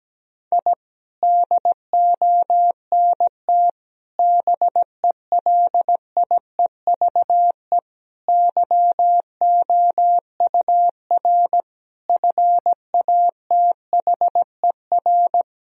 Morse code, heard as a keyed tone, about 17 wpm, 700 Hz. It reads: I DONT BELIEVE YOUR FATHER
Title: Sentences from Top 1000 Words - Review Part 3 - 17wpm